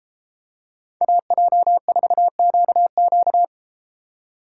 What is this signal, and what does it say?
Morse code, heard as a keyed tone, about 30 wpm, 700 Hz. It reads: AJ4QQ